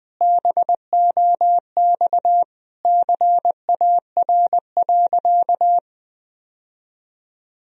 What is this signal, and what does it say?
Morse code, heard as a keyed tone, about 20 wpm, 700 Hz. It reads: BOX CAR.
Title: Sentences from Top 1000 Words - Review Part 2 - 20wpm